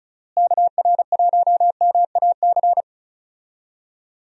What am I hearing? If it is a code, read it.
Morse code, heard as a keyed tone, about 36 wpm, 700 Hz. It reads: KR1MAC
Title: USA Call Signs - 6 Characters - Repeated 3x - 35wpm